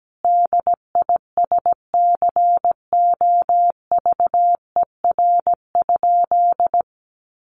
Morse code, heard as a keyed tone, about 17 wpm, 700 Hz. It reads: DISCOVER?
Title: Sentences from Top 1100 Words 17wpm